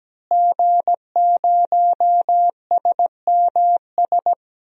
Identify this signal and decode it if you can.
Morse code, heard as a keyed tone, about 17 wpm, 700 Hz. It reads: G0SMS
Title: DXCC-100 Call Signs - 5 Characters - Repeated 2x - 17wpm